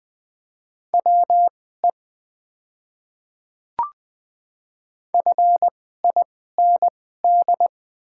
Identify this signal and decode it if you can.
Morse code, heard as a keyed tone, about 21 wpm, spaced as for 15 wpm, 700 Hz. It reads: WE FIND
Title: 2x Character Spacing - Top 200 Words - 20wpm